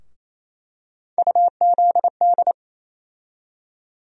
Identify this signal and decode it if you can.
Morse code, heard as a keyed tone, about 28 wpm, 700 Hz. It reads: UZD
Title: USA Call Sign Postfixes - 3 Letters 28wpm